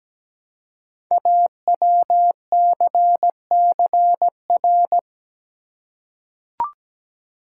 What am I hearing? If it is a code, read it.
Morse code, heard as a keyed tone, about 17 wpm, 700 Hz. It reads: AWCCR
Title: Code Groups - 5 Letters - 17wpm